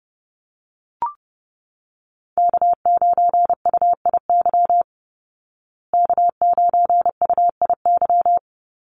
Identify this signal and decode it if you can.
Morse code, heard as a keyed tone, about 28 wpm, 700 Hz. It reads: K9UIY K9UIY
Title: Call Signs Repeated 2x - 30wpm